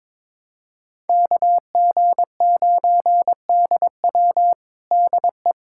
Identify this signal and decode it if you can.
Morse code, heard as a keyed tone, about 21 wpm, 700 Hz. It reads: KG9DW DE
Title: USA Callsign DE Callsign 22wpm